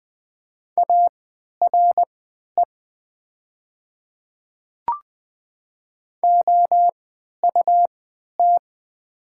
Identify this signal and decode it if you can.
Morse code, heard as a keyed tone, about 20 wpm, spaced as for 11 wpm, 700 Hz. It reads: ARE OUT
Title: 3x Character Spacing - Top 100 Words - 20wpm